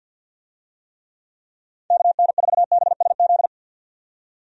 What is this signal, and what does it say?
Morse code, heard as a keyed tone, about 52 wpm, 700 Hz. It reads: KN4BSZ
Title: USA Call Signs - 6 Characters - 50wpm